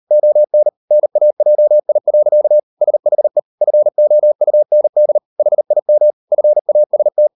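Morse code, heard as a keyed tone, about 38 wpm, 600 Hz. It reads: ON NAJI. SHE FOUND HIM FAST
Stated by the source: Morse code practice at 38 WPM, sending Ken Follett's novel Never in English, from CW med Gnister.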